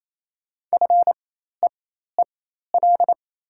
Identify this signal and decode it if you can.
Morse code, heard as a keyed tone, about 28 wpm, spaced as for 13 wpm, 700 Hz. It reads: FEEL
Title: 4x Character Spacing - Top 500 Words - 28wpm